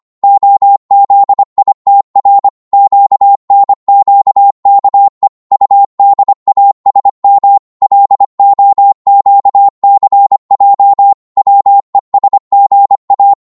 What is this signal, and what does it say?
Morse code, heard as a keyed tone, about 25 wpm, 800 Hz. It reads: OZITR QNQKE UDASM LOQCJ WESGA